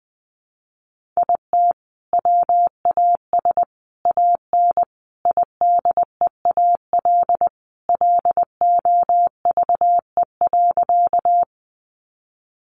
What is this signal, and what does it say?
Morse code, heard as a keyed tone, about 20 wpm, 700 Hz. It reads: IT WAS AN IDEAL LOVE.
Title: Sentences from Top 1400 Words 20wpm